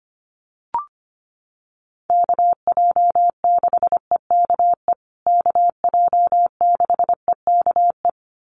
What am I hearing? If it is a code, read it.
Morse code, heard as a keyed tone, about 25 wpm, 700 Hz. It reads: KJ6EKE KJ6EKE
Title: USA Call Signs Repeated 2x 25wpm